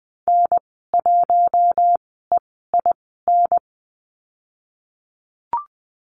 Morse code, heard as a keyed tone, about 20 wpm, spaced as for 15 wpm, 700 Hz. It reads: N1EIN
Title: 048 Lesson 9g - Number 4 - Call Signs 20wpm